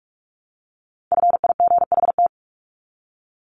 Morse code, heard as a keyed tone, about 44 wpm, 700 Hz. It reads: FIGHT
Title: ING Suffix - Top 100 Words with Base Words 45wpm